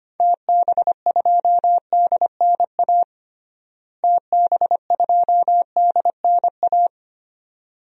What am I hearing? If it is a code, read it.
Morse code, heard as a keyed tone, about 25 wpm, 700 Hz. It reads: TB2DNA TB2DNA
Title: DXCC-100 Call Signs - 6 Characters - Repeated 2x - 25wpm